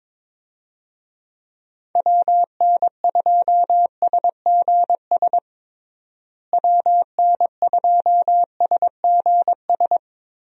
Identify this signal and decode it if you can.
Morse code, heard as a keyed tone, about 22 wpm, 700 Hz. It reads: WN2SGS WN2SGS
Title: USA Call Signs - 6 Characters - Repeated 3x - 22wpm